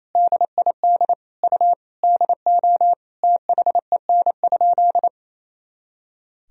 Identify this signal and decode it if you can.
Morse code, heard as a keyed tone, about 28 wpm, 700 Hz. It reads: DID U DO THEN?
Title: Sentences from Top 100 Words and Abbreviations 28wpm